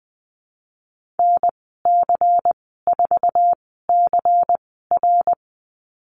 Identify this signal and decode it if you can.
Morse code, heard as a keyed tone, about 19 wpm, spaced as for 14 wpm, 700 Hz. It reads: NC4CR
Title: 147 Lesson 24g - Letter y - Call Signs 20wpm